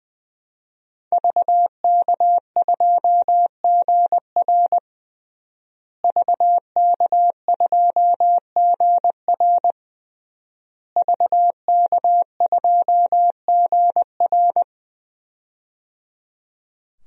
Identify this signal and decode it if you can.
Morse code, heard as a keyed tone, about 20 wpm, 700 Hz. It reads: VK2GR VK2GR VK2GR
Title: DXCC-100 Call Signs - 5 Characters - Repeated 3x - 20wpm